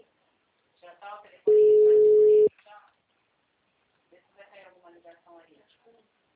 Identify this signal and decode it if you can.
Ringback tone (Europe, 425 Hz), with speech.